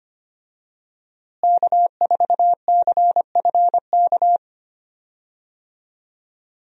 Morse code, heard as a keyed tone, about 25 wpm, 700 Hz. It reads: K4CFK
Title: USA Call Signs - 5 Characters - Repeated 3x - 25wpm